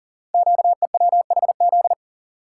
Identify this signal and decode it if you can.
Morse code, heard as a keyed tone, about 39 wpm, 700 Hz. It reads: QEWHZ
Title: Code Groups - 5 Letter:Numbers - 40wpm